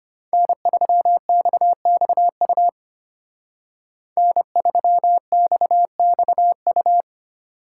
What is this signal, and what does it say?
Morse code, heard as a keyed tone, about 33 wpm, 700 Hz. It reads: N3XXU N3XXU